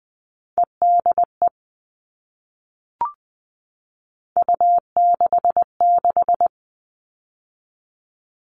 Morse code, heard as a keyed tone, about 19 wpm, 700 Hz. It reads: EDE U66
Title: All Cut Numbers - 3-Digit 20wpm